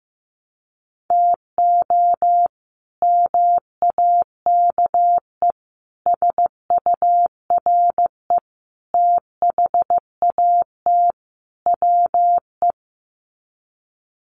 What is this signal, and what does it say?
Morse code, heard as a keyed tone, about 15 wpm, 700 Hz. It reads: TO MAKE SURE THAT WE
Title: Sets of 5 Words Encore 15wpm